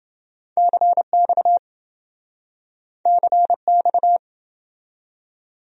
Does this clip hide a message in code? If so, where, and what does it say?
Yes: Morse code from 0.6 to 4.2 s, heard as a keyed tone, about 27 wpm, 700 Hz. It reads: CX CX